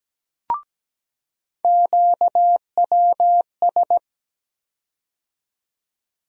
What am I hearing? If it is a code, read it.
Morse code, heard as a keyed tone, about 17 wpm, 700 Hz. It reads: QWS
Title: USA Call Sign Postfixes - 3 Letters 17wpm